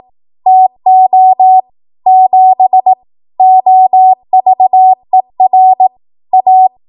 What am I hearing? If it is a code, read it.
Morse code, heard as a keyed tone, about 18 wpm, 750 Hz. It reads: TO 7 OVER A
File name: Archive / 18 WPM / 18WPM